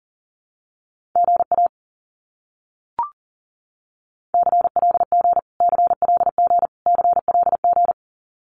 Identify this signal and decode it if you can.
Morse code, heard as a keyed tone, about 37 wpm, 700 Hz. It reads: GA CLG CLG CLG
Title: Abbreviations Repeated 3x - 40wpm